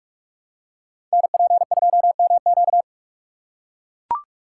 Morse code, heard as a keyed tone, about 44 wpm, 700 Hz. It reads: NP2MQ
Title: Code Groups - 5 Letter:Numbers - 45wpm